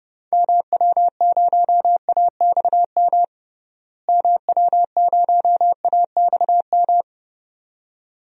Morse code, heard as a keyed tone, about 30 wpm, 700 Hz. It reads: MW0AXM MW0AXM
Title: DXCC-100 Call Signs - 6 Characters - Repeated 2x - 30wpm